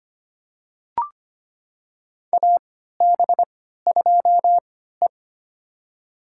Morse code, heard as a keyed tone, about 25 wpm, spaced as for 14 wpm, 700 Hz. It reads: AB2E
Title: USA Call Signs - 4 Characters - 3x Character Spacing - 25wpm